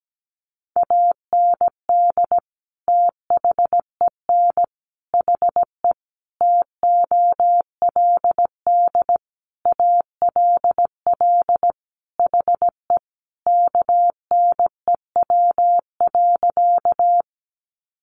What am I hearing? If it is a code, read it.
Morse code, heard as a keyed tone, about 17 wpm, 700 Hz. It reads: AND THEN HE TOLD ALL HE KNEW.
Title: Sentences from Top 400 Words 17wpm